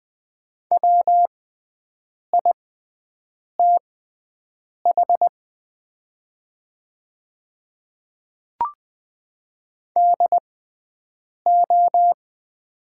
Morse code, heard as a keyed tone, about 20 wpm, spaced as for 7 wpm, 700 Hz. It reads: WITH DO
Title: 6x Character Spacing - Top 100 Words - 20wpm